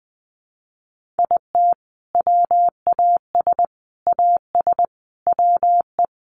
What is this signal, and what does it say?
Morse code, heard as a keyed tone, about 20 wpm, 700 Hz. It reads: IT WAS AS WE
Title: Sentences from Top 1500 Words 20wpm